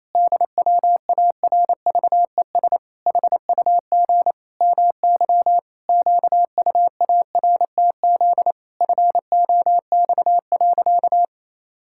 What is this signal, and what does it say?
Morse code, heard as a keyed tone, about 28 wpm, 700 Hz. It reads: DWARVES HUG MY QUARTZ FOX.
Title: Pangram Challenge v2 - 28wpm